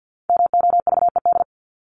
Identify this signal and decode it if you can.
Morse code, heard as a keyed tone, about 50 wpm, 700 Hz. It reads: MOVED
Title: ED Suffix - Top 100 Words with Base Words 50wpm